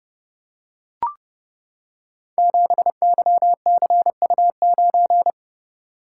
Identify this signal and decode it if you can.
Morse code, heard as a keyed tone, about 31 wpm, 700 Hz. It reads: 7YCU9
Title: Code Groups - 5 Letter:Numbers - 30wpm